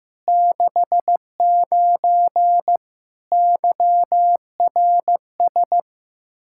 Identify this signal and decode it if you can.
Morse code, heard as a keyed tone, about 15 wpm, 700 Hz. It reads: 69 YRS
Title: K1USN SST QSO Practice 15wpm